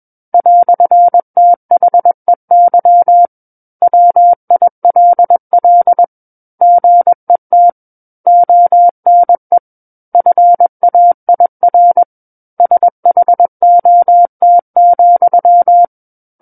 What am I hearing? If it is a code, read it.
Morse code, heard as a keyed tone, about 21 wpm, 700 Hz. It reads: "THEY WILL GET ONE FAIR SHOT,
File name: Book One / 12-Chapter 12 - 21 WPM